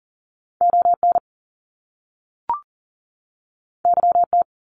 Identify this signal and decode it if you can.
Morse code, heard as a keyed tone, about 38 wpm, 700 Hz. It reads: ON YT
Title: Canadian Provinces and Territories 40wpm